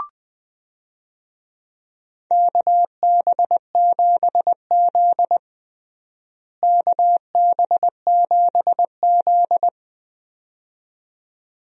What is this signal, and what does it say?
Morse code, heard as a keyed tone, about 21 wpm, 700 Hz. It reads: KB7Z KB7Z
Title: USA Call Signs - 4 Characters - Repeated 2x - 20wpm